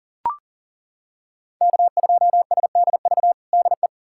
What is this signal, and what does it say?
Morse code, heard as a keyed tone, about 40 wpm, 700 Hz. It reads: K2SDV DE
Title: USA Callsign DE Callsign - 5 Characters - 40wpm